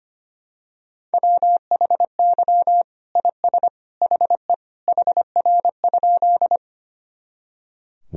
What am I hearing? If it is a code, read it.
Morse code, heard as a keyed tone, about 25 wpm, 700 Hz. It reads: WHY IS HE HR?
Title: Sentences from Top 200 Words and Abbreviations 25wpm